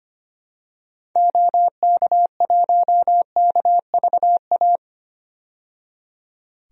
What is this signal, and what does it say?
Morse code, heard as a keyed tone, about 25 wpm, 700 Hz. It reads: OK1KVA
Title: DXCC-100 Call Signs - 6 Characters - 25wpm